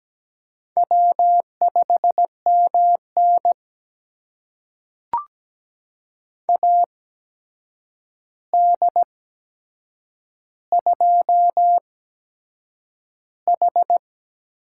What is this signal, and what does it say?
Morse code, heard as a keyed tone, about 17 wpm, spaced as for 7 wpm, 700 Hz. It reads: W5MN AD2H
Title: USA Call Signs - 4 Characters - 8x Character Spacing - 17wpm